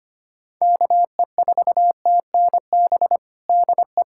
Morse code, heard as a keyed tone, about 25 wpm, 700 Hz. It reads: KE4TNB DE
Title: Practice QSO - Exchange 4 - Ending - 25wpm